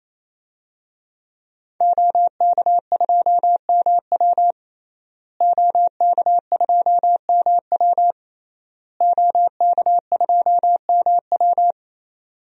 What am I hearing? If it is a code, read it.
Morse code, heard as a keyed tone, about 27 wpm, 700 Hz. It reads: OK2MW OK2MW OK2MW
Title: DXCC-100 Call Signs - 5 Characters - Repeated 3x - 28wpm